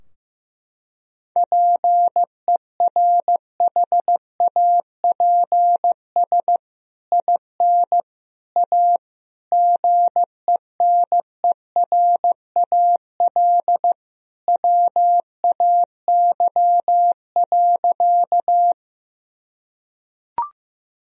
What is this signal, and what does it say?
Morse code, heard as a keyed tone, about 15 wpm, 700 Hz. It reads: PERHAPS IN A GENERAL WAY.
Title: Sentences from Top 1000 Words - Review Part 3 - 15wpm